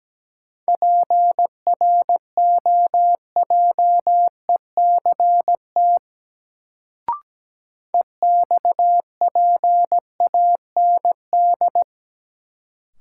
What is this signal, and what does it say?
Morse code, heard as a keyed tone, about 17 wpm, 700 Hz. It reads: PROJECT EXPAND
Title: Sentences from Top 1400 Words - New Word Drill - 17wpm